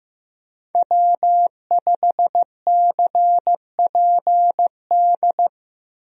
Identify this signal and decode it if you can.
Morse code, heard as a keyed tone, about 16 wpm, 700 Hz. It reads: W5CPD
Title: USA Call Signs - 5 Characters - Repeated 3x - 15wpm